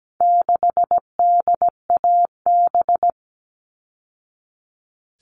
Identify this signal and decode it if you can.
Morse code, heard as a keyed tone, about 17 wpm, 700 Hz. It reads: 6DAB
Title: All Cut Numbers - 4-Digit 17wpm